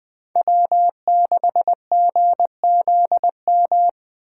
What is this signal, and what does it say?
Morse code, heard as a keyed tone, about 20 wpm, 700 Hz. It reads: W6GZM
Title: Speed Racing - USA 1x3 Call Signs 17wpm